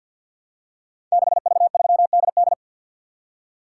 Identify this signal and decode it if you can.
Morse code, heard as a keyed tone, about 51 wpm, 700 Hz. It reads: 6V3DD
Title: Code Groups - 5 Letter:Numbers - 50wpm